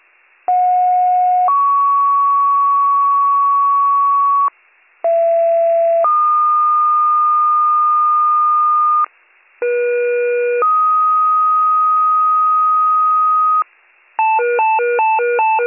Two-tone page, heard as a one-second tone, then a three-second tone, about 730 then 1120 Hz, about 670 then 1220 Hz, about 480 then 1250 Hz. It is heard from 0.5 to 13.6 s.